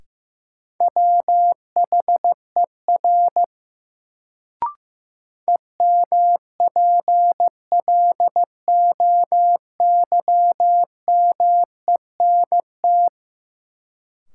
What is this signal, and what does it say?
Morse code, heard as a keyed tone, about 15 wpm, 700 Hz. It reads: WHER EMPLOYMENT
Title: Sentences from Top 1500 Words - New Word Drill Alt - 15wpm